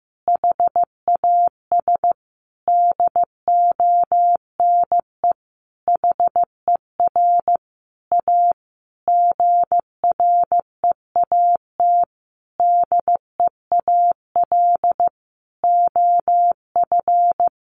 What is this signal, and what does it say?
Morse code, heard as a keyed tone, about 15 wpm, 700 Hz. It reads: HAS DONE HER A GREAT DEAL OF
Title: Sentences from Top 900 Words 15wpm